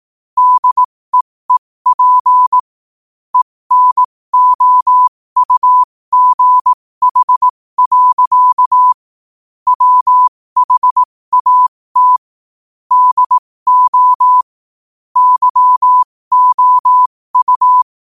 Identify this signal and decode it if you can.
Morse code, heard as a keyed tone, about 18 wpm, spaced as for 15 wpm, 1000 Hz. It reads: DEEP ENOUGH. WHAT DO YOU
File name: qotdfiles / 15wpm